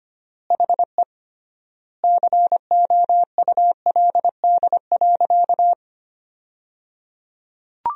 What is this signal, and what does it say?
Morse code, heard as a keyed tone, about 25 wpm, 700 Hz. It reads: HE COULD.
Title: 3x Word-Spacing - Sentences from Top 500 Words - 25wpm